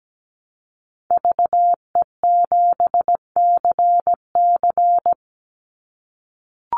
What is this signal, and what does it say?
Morse code, heard as a keyed tone, about 17 wpm, 700 Hz. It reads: VE7CC
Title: Call Signs - 17wpm